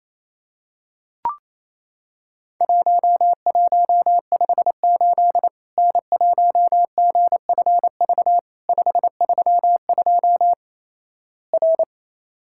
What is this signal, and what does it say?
Morse code, heard as a keyed tone, about 28 wpm, 700 Hz. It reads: 1158 N1GFV 532 R
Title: Worked All Europe DX Contest - QTC Exchange - 28wpm